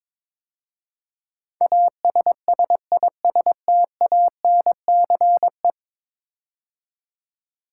Morse code, heard as a keyed tone, about 22 wpm, 700 Hz. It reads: ASSISTANCE
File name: practice / 22wpm / 10 Letter Words - First 3 Letters Given - 22wpm